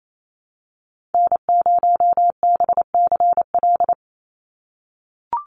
Morse code, heard as a keyed tone, about 28 wpm, 700 Hz. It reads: N0BCL
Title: USA Call Signs 28wpm